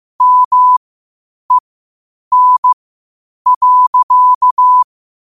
Morse code, heard as a keyed tone, about 15 wpm, spaced as for 8 wpm, 1000 Hz. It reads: MEN.
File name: qotdfiles / 8wpm